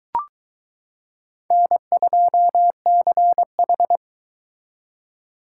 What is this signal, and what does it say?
Morse code, heard as a keyed tone, about 22 wpm, 700 Hz. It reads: N2CH